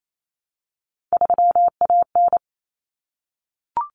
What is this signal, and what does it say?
Morse code, heard as a keyed tone, about 28 wpm, 700 Hz. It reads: 3AN
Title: Common Cut Numbers - 3-Digit 28wpm